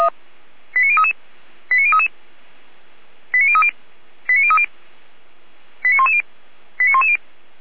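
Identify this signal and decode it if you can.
Five-tone selective call (ZVEI), heard as quick runs of five tones, tones of about 70 ms. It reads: 89033 89033 89030 89030 89110 89110
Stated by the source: ZVEI five-tone call received by SDR.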